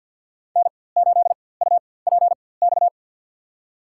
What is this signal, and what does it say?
Morse code, heard as a keyed tone, about 47 wpm, spaced as for 23 wpm, 700 Hz. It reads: N8UPX
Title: USA Call Signs - 5 Characters - 4x Character Spacing - 50wpm